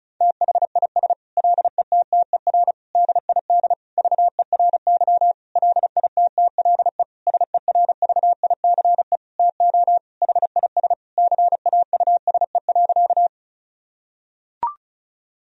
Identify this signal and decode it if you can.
Morse code, heard as a keyed tone, about 36 wpm, 700 Hz. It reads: THIS LETTER DID VERY LITTLE SERVICE TO HIS CAUSE.
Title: Sentences from Top 1000 Words - Review Part 4 - 35wpm